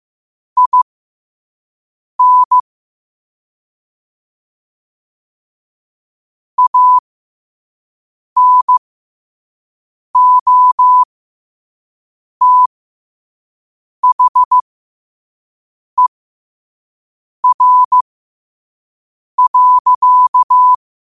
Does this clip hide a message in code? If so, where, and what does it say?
Yes: Morse code from 0.6 to 20.7 s, heard as a keyed tone, about 15 wpm, spaced as for 5 wpm, 1000 Hz. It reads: IN ANOTHER.